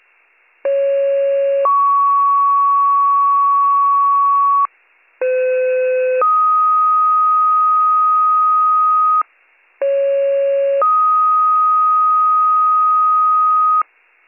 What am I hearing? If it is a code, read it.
Two-tone page, heard as a one-second tone, then a three-second tone, about 550 then 1120 Hz, about 510 then 1250 Hz, about 550 then 1250 Hz. It is heard from 0.6 to 13.8 s.